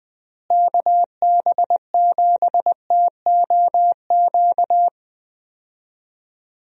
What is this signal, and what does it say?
Morse code, heard as a keyed tone, about 20 wpm, 700 Hz. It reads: KB7TOQ
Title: USA Call Signs - 6 Characters - Repeated 2x - 20wpm